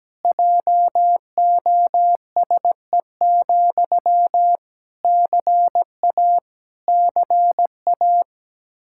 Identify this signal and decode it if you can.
Morse code, heard as a keyed tone, about 17 wpm, 700 Hz. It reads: JOSE, CA CA
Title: Practice QSO - Exchange 1 - Beginning - 17wpm